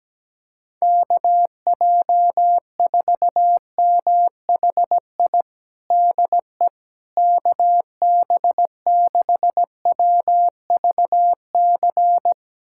Morse code, heard as a keyed tone, about 17 wpm, 700 Hz. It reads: KJ4MHI DE KB6WVC
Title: USA Callsign DE Callsign - 6 Characters - 17wpm